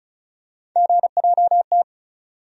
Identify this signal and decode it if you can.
Morse code, heard as a keyed tone, about 34 wpm, 700 Hz. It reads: GJT
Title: USA Call Sign Postfixes - 3 Letters 35wpm